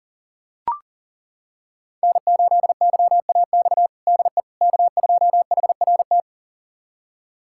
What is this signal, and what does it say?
Morse code, heard as a keyed tone, about 40 wpm, 700 Hz. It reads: N8YAX DE K2HRT
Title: USA Callsign DE Callsign - 5 Characters - 40wpm